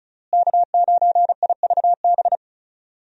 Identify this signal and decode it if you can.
Morse code, heard as a keyed tone, about 36 wpm, 700 Hz. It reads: K9IVB
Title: USA Call Signs - 5 Characters - Repeated 3x - 35wpm